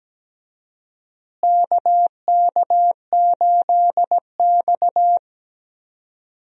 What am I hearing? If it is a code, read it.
Morse code, heard as a keyed tone, about 17 wpm, 700 Hz. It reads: KK8X
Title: USA Call Signs - 4 Characters - Repeated 2x - 17wpm